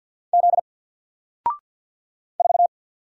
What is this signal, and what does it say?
Morse code, heard as a keyed tone, about 50 wpm, 700 Hz. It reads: Z 4
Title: Single Letter-Number 50wpm